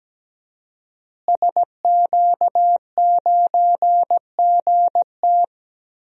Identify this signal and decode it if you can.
Morse code, heard as a keyed tone, about 17 wpm, 700 Hz. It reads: SQ9GT
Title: DXCC-100 Call Signs - 5 Characters - Repeated 3x - 17wpm